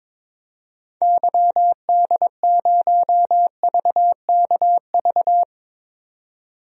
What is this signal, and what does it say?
Morse code, heard as a keyed tone, about 22 wpm, 700 Hz. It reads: YD0VKV
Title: DXCC-100 Call Signs - 6 Characters - 22wpm